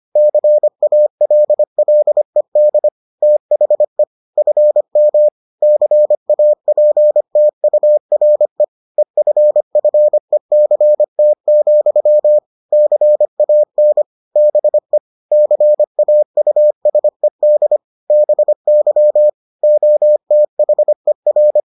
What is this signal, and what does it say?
Morse code, heard as a keyed tone, about 25 wpm, 600 Hz. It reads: CALLED THE FM CAPTURE EFFECT, CAN BE CAUSED BY OTHER